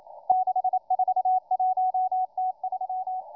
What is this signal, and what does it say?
Morse code, heard as a keyed tone, about 28 wpm, 740 Hz. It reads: 641T3